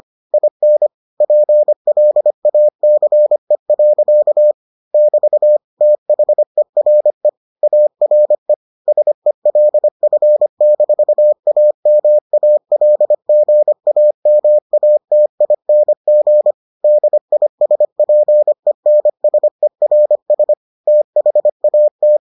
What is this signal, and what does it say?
Morse code, heard as a keyed tone, about 25 wpm, 600 Hz. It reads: IN PLACE. <BT> THERE ARE SELF-AMALGAMATING DISPENSERS THAT